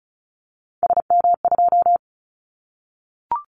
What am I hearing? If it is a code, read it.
Morse code, heard as a keyed tone, about 34 wpm, 700 Hz. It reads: SM2
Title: Callsign Prefixes for Top 25 DXCC - 35wpm